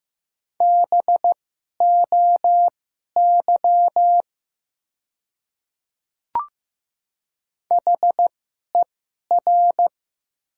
Morse code, heard as a keyed tone, about 15 wpm, spaced as for 11 wpm, 700 Hz. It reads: BOY HER
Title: 2x Character Spacing - Top 200 Words - 15wpm